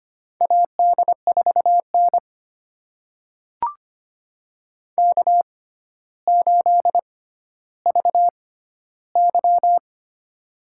Morse code, heard as a keyed tone, about 25 wpm, spaced as for 13 wpm, 700 Hz. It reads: AD4N K8VY
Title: USA Call Signs - 4 Characters - 6x Character Spacing - 25wpm